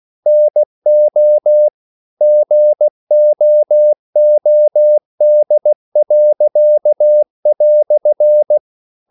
Morse code, heard as a keyed tone, about 16 wpm, 600 Hz. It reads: NO GOOD."
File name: CW / Practice / EnglishQuotations0216